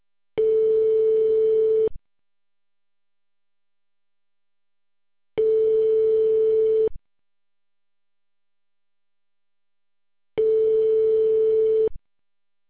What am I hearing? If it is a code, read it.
Ringback tone (France), 440 Hz.